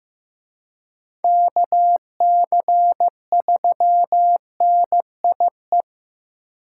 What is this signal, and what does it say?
Morse code, heard as a keyed tone, about 15 wpm, 700 Hz. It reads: KC3NIE
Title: K1USN SST QSO Practice - 15wpm